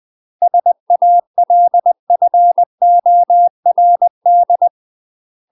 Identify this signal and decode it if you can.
Morse code, heard as a keyed tone, about 20 wpm, 700 Hz. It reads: SALFORD
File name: UK MASTER QSOs TRAINING / 20WPM / UK MASTER QSOs 0023